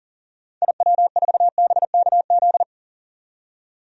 Morse code, heard as a keyed tone, about 40 wpm, 700 Hz. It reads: IW4BKZ
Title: DXCC-100 Call Signs - 6 Characters - 40wpm